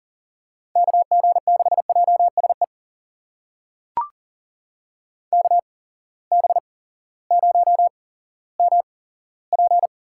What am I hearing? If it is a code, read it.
Morse code, heard as a keyed tone, about 39 wpm, spaced as for 18 wpm, 700 Hz. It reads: KG6JSE KB0MP
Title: USA Call Signs - 6 Characters - 8x Character Spacing - 40wpm